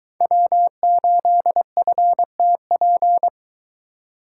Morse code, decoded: W8FTP

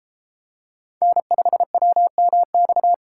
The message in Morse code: N5WMX

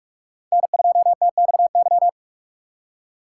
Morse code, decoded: N2TXY

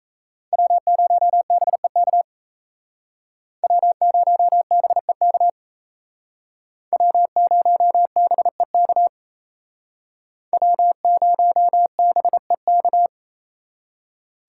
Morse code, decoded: W0BEK W0BEK W0BEK W0BEK

